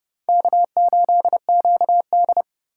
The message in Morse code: K8QD

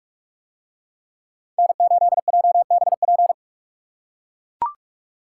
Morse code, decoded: N8JBP

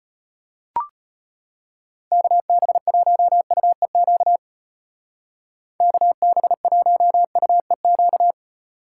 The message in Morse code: KB1UEQ KB1UEQ

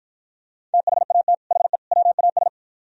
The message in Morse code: THAT HE WAS

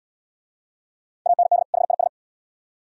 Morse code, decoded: AND HIS

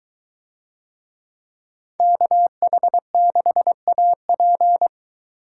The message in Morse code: KH6AP

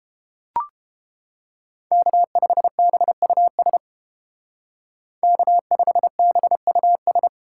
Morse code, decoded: K5BUS K5BUS

